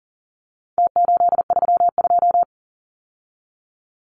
Morse code, heard as a keyed tone, about 40 wpm, 700 Hz. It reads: T832
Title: Numbers - Contest Style - 4-Digit 40wpm